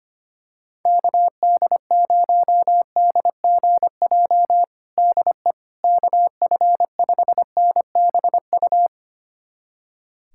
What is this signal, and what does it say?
Morse code, heard as a keyed tone, about 25 wpm, 700 Hz. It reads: KD0DGJ DE KF5NBU
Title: USA Callsign DE Callsign - 6 Characters - 25wpm